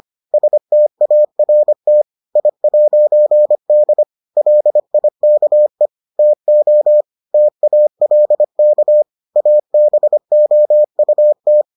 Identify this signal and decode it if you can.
Morse code, heard as a keyed tone, about 25 wpm, 600 Hz. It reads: START I'D LIKE TO TALK ABOUT